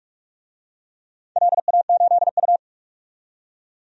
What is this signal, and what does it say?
Morse code, heard as a keyed tone, about 44 wpm, 700 Hz. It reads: LA8U
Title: DXCC-100 Call Signs - 4 Characters - 45wpm